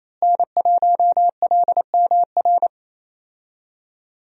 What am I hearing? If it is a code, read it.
Morse code, heard as a keyed tone, about 28 wpm, 700 Hz. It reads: N1LMR
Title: USA Call Signs - 5 Characters - Repeated 2x - 28wpm